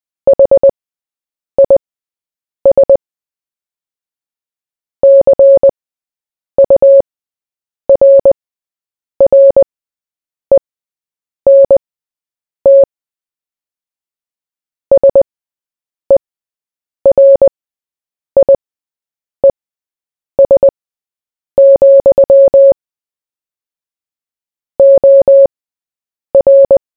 Morse code, heard as a keyed tone, about 20 wpm, spaced as for 8 wpm, 550 Hz. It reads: HIS CURRENT SERIES, OR
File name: Short-Story-4-8wpm